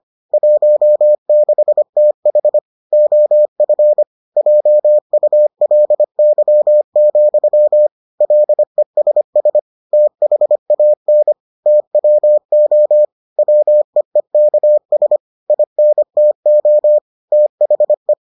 Morse code, heard as a keyed tone, about 25 wpm, 600 Hz. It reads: 16TH OF JULY, LESS THAN TWO WEEKS INTO THE